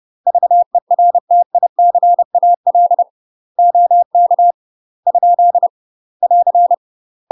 Morse code, heard as a keyed tone, about 30 wpm, 700 Hz. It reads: VERTICAL OK ? <AR>
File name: UK MASTER QSOs TRAINING/30WPM